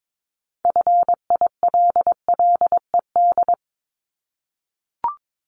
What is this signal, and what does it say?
Morse code, heard as a keyed tone, about 22 wpm, 700 Hz. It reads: FILLED
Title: Multi-letter Phonemes 22wpm